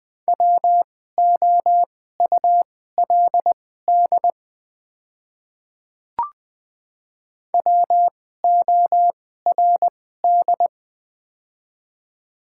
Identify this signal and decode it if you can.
Morse code, heard as a keyed tone, about 20 wpm, spaced as for 14 wpm, 700 Hz. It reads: WOULD WORD
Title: 2x Character Spacing - Top 100 Words - 20wpm